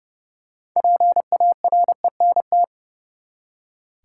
Morse code, heard as a keyed tone, about 30 wpm, 700 Hz. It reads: PARENT
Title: Six Letter Words - 30wpm